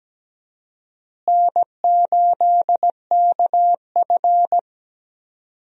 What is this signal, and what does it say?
Morse code, heard as a keyed tone, about 17 wpm, 700 Hz. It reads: N8KF
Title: USA Call Signs - 4 Characters - Repeated 2x - 17wpm